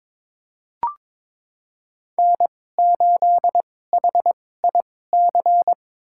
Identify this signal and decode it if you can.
Morse code, heard as a keyed tone, about 22 wpm, spaced as for 16 wpm, 700 Hz. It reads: N8HIC
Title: USA Call Signs - 5 Characters - 2x Character Spacing - 22wpm